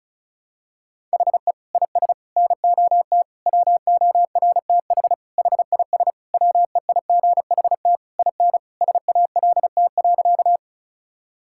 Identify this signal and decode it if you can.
Morse code, heard as a keyed tone, about 36 wpm, 700 Hz. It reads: HE IS NOT WORTH HIS WEIGHT IN SALT.
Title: Sentences from Top 1000 Words - Review Part 5 - 35wpm